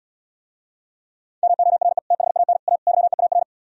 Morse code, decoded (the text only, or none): MORE THAN A YEAR